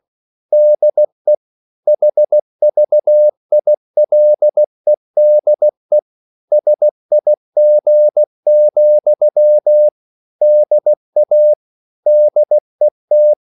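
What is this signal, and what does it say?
Morse code, heard as a keyed tone, about 16 wpm, 600 Hz. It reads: DE HVILEDE SIG, DA DET
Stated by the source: Slow (QRS) Morse code at 16 wpm, from CW med Gnister.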